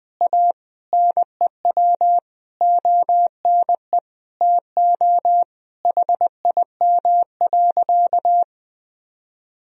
Morse code, heard as a keyed tone, about 20 wpm, 700 Hz. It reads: A NEW ONE TO HIM.
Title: Sentences from Top 1000 Words - Review Part 4 - 20wpm